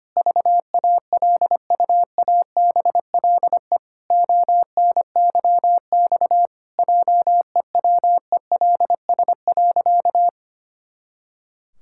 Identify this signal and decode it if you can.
Morse code, heard as a keyed tone, about 25 wpm, 700 Hz. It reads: VALUABLE ONYX JEWELS.
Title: Pangram Challenge v2 - 25wpm